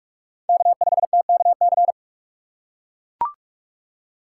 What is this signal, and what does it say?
Morse code, heard as a keyed tone, about 42 wpm, 700 Hz. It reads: K5TKC